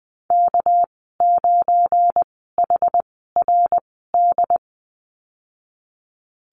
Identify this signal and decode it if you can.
Morse code, heard as a keyed tone, about 20 wpm, spaced as for 15 wpm, 700 Hz. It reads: K9HRD
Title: 200 Lesson 32g - Letter k - Call Signs 20wpm